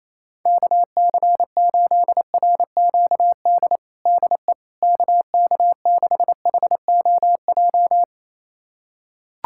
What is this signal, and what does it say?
Morse code, heard as a keyed tone, about 28 wpm, 700 Hz. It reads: KC8RQD DE KK6HOJ
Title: USA Callsign DE Callsign - 6 Characters - 28wpm